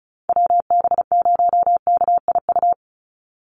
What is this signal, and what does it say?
Morse code, heard as a keyed tone, about 34 wpm, 700 Hz. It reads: WB0KIU
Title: NA SOTA Chaser Callsigns 35wpm